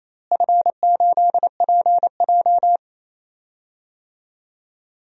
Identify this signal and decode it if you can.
Morse code, heard as a keyed tone, about 28 wpm, 700 Hz. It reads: F8PJ
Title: DXCC-100 Call Signs - 4 Characters - Repeated 2x - 28wpm